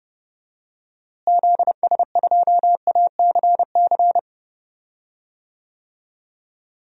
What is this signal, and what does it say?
Morse code, heard as a keyed tone, about 32 wpm, 700 Hz. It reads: ZS2ACC
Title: DXCC-100 Call Signs - 6 Characters - Repeated 3x - 30wpm